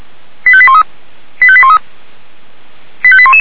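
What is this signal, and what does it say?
Five-tone selective call (ZVEI), heard as quick runs of five tones, tones of about 70 ms. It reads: 85813 85813 85811
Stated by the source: Received by SDR.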